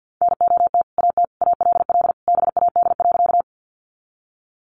Morse code, heard as a keyed tone, about 50 wpm, 700 Hz. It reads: NOT AT ALL BAD.
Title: Sentences from Top 700 Words 50wpm